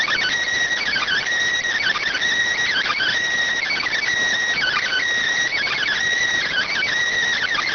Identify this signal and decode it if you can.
Emergency Alert System (SAME) tones (header bursts), heard as harsh buzzing bursts. It is heard from 1.1 to 6.6 s.